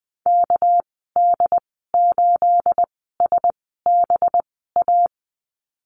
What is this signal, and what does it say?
Morse code, heard as a keyed tone, about 20 wpm, spaced as for 15 wpm, 700 Hz. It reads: KD8SBA
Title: 245 Lesson 39g - Letter z - Call Signs 20wpm